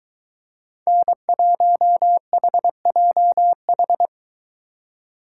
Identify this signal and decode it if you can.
Morse code, heard as a keyed tone, about 22 wpm, 700 Hz. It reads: N1HJH